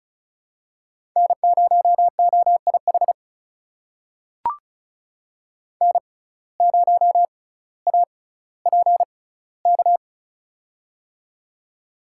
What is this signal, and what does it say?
Morse code, heard as a keyed tone, about 36 wpm, spaced as for 18 wpm, 700 Hz. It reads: N0OIH N0APK
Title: USA Call Signs - 5 Characters - 6x Character Spacing - 35wpm